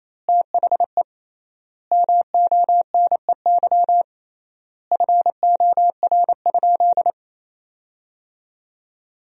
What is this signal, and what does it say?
Morse code, heard as a keyed tone, about 28 wpm, 700 Hz. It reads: THE MONEY FOR?
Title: 3x Word-Spacing - Sentences from Top 400 Words - 28wpm